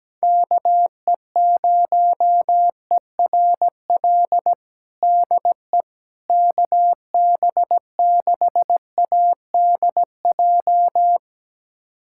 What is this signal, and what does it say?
Morse code, heard as a keyed tone, about 17 wpm, 700 Hz. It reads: KE0ERL DE KB6ADJ
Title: USA Callsign DE Callsign 17wpm